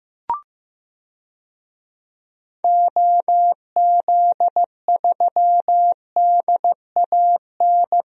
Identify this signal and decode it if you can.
Morse code, heard as a keyed tone, about 16 wpm, 700 Hz. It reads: OZ3DAN
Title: DXCC-100 Call Signs - 6 Characters - 15wpm